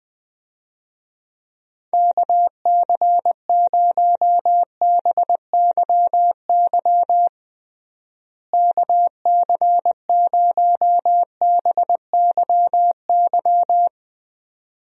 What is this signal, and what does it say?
Morse code, heard as a keyed tone, about 20 wpm, 700 Hz. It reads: KC0BYY KC0BYY
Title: USA Call Signs - 6 Characters - Repeated 2x - 20wpm